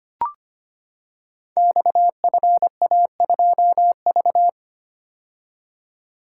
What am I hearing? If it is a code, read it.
Morse code, heard as a keyed tone, about 25 wpm, 700 Hz. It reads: XFA2V
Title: Code Groups - 5 Letter:Numbers - 25wpm